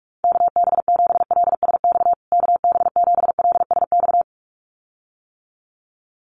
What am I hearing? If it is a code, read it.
Morse code, heard as a keyed tone, about 44 wpm, 700 Hz. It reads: KB7LSX KB7LSX
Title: USA Call Signs Repeated 2x 45wpm